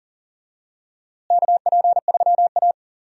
Morse code, heard as a keyed tone, about 40 wpm, 700 Hz. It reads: KP3A
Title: USA Call Signs - 4 Characters - Repeated 2x - 40wpm